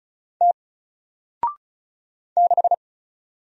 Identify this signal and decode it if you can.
Morse code, heard as a keyed tone, about 36 wpm, 700 Hz. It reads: T 6
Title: Single Letter-Number Alt 35wpm